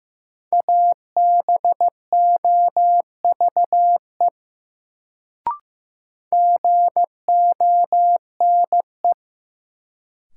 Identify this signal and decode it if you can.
Morse code, heard as a keyed tone, about 15 wpm, 700 Hz. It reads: ABOVE GONE
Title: Sentences from Top 1200 Words - New Word Drill Alt - 15wpm